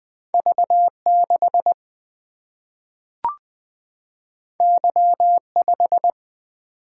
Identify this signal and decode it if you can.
Morse code, heard as a keyed tone, about 20 wpm, 700 Hz. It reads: V6 Y5
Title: Single-Letter Single-Number 20wpm